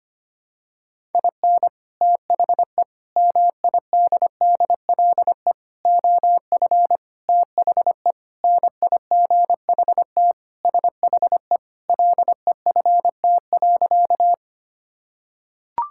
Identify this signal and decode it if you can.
Morse code, heard as a keyed tone, about 25 wpm, 700 Hz. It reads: IN THE MIDDLE OF THE NIGHT SHE LEFT.
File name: practice / 25wpm / Sentences with 3-Word Phrases - 25wpm